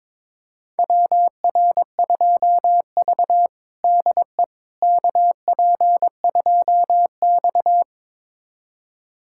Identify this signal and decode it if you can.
Morse code, heard as a keyed tone, about 21 wpm, 700 Hz. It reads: WR2V DE KP2X
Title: USA Callsign DE Callsign - 4 Characters - 22wpm